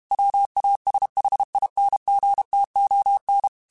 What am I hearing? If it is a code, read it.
Morse code, heard as a keyed tone, about 32 wpm, 800 Hz. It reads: WASHINGTON
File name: QSOs / QSOs-mp3s / WD0BGP 1@30